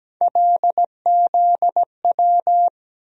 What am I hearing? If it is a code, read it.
Morse code, heard as a keyed tone, about 17 wpm, 700 Hz. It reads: LZW
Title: USA Call Sign Postfixes - 3 Letters 17wpm